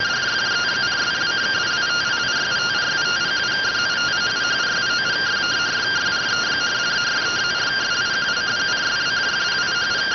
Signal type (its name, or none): two-tone teleprinter signal